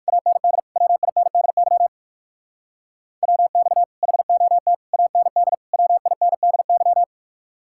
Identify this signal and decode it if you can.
Morse code, heard as a keyed tone, about 56 wpm, 700 Hz. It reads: AND WINDY WX HOT AND WINDY